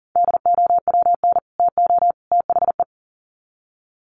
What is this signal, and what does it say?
Morse code, heard as a keyed tone, about 40 wpm, 700 Hz. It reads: DOWN TO THE
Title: Sets of 3 Words 40wpm